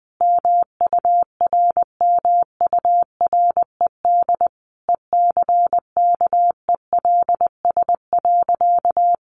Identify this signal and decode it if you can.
Morse code, heard as a keyed tone, about 20 wpm, 700 Hz. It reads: MURMURED ECKELS.
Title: A Sound Of Thunder 20wpm